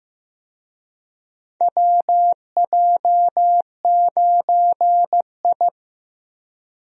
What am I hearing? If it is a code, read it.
Morse code, heard as a keyed tone, about 15 wpm, 700 Hz. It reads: WJ9I
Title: USA Call Signs - 4 Characters - Repeated 3x - 15wpm